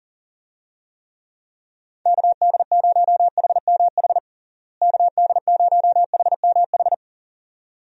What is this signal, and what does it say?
Morse code, heard as a keyed tone, about 40 wpm, 700 Hz. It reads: KD0HMH KD0HMH
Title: USA Call Signs - 6 Characters - Repeated 2x - 40wpm